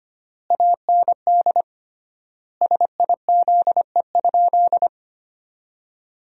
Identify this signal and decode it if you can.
Morse code, heard as a keyed tone, about 25 wpm, 700 Hz. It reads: AND SIZE?
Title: 3x Word-Spacing - Sentences from Top 500 Words - 25wpm